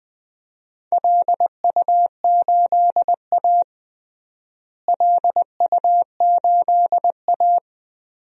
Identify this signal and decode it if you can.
Morse code, heard as a keyed tone, about 21 wpm, 700 Hz. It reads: LU8A LU8A
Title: DXCC-100 Call Signs - 4 Characters - Repeated 3x - 20wpm